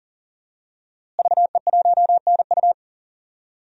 Morse code, heard as a keyed tone, about 40 wpm, 700 Hz. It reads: VE1NU
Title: DXCC-100 Call Signs - 5 Characters - Repeated 2x - 40wpm